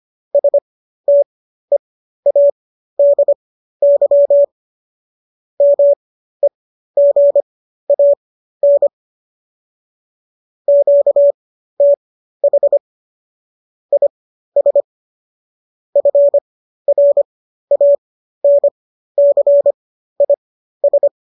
Morse code, heard as a keyed tone, about 25 wpm, spaced as for 13 wpm, 570 Hz. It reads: STEADY MEGAN QTH IS FRANCIS